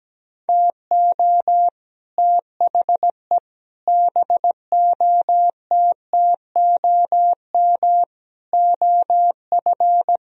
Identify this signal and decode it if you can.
Morse code, heard as a keyed tone, about 18 wpm, 700 Hz. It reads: TO THE BOTTOM OF
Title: Speed-Racing - Sets of 5 Words 17wpm